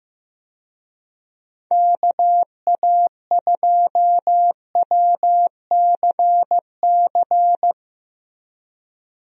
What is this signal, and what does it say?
Morse code, heard as a keyed tone, about 16 wpm, 700 Hz. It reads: KA2WCC
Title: USA Call Signs - 6 Characters - Repeated 3x - 15wpm